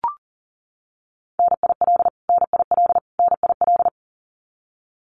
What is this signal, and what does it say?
Morse code, heard as a keyed tone, about 40 wpm, 700 Hz. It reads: NIL NIL NIL
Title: Abbreviations Repeated 3x - 40wpm